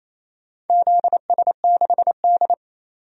Morse code, heard as a keyed tone, about 28 wpm, 700 Hz. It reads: ZS6D
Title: DXCC-100 Call Signs - 4 Characters - 28wpm